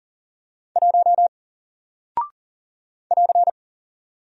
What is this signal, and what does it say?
Morse code, heard as a keyed tone, about 40 wpm, 700 Hz. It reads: 1 <AR>
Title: Single Character (Letter:Number:Punctuation:Prosign) Alt 40wpm